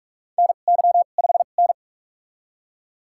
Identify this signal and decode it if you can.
Morse code, heard as a keyed tone, about 46 wpm, spaced as for 33 wpm, 700 Hz. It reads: NY5N
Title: USA Call Signs - 2x Character Spacing - 45wpm